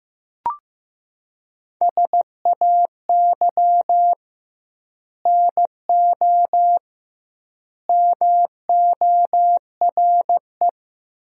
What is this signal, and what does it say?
Morse code, heard as a keyed tone, about 15 wpm, 700 Hz. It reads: SAY NO MORE
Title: 2x Word-Spacing - Sentences from Top 200 Words - 15wpm